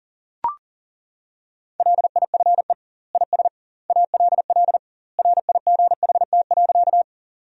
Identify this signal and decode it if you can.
Morse code, heard as a keyed tone, about 40 wpm, 700 Hz. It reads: LIFE IS ALL RIGHT.
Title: 2x Word-Spacing - Sentences from Top 300 Words - 40wpm